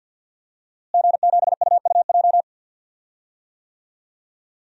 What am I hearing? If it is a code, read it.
Morse code, heard as a keyed tone, about 53 wpm, 700 Hz. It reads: G7UUJ
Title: DXCC-100 Call Signs - 5 Characters - Repeated 2x - 50wpm